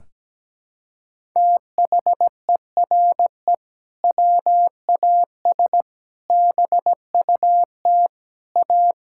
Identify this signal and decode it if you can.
Morse code, heard as a keyed tone, about 17 wpm, 700 Hz. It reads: THERE WAS BUT A
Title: Sentences from Top 1000 Words - Review Part 5 - 17wpm